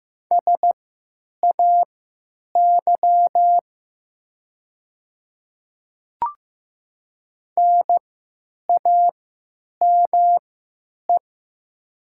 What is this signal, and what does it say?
Morse code, heard as a keyed tone, about 15 wpm, spaced as for 9 wpm, 700 Hz. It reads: SAY NAME
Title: 3x Character Spacing - Top 300 Words - 15wpm